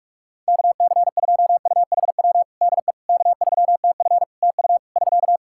Morse code, heard as a keyed tone, about 46 wpm, 700 Hz. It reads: KC2UHW DE K3TF TU <SK>
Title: Practice QSO - Exchange 4 - Ending - 45wpm